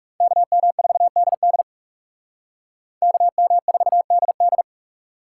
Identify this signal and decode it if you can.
Morse code, heard as a keyed tone, about 38 wpm, 700 Hz. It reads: KM4DD KM4DD